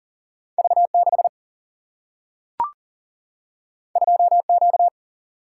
Morse code, heard as a keyed tone, about 42 wpm, 700 Hz. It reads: V 6 2 Q
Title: Single-Letter Single-Number 40wpm